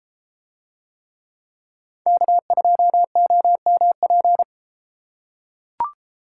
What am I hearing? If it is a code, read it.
Morse code, heard as a keyed tone, about 32 wpm, 700 Hz. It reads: K2OMP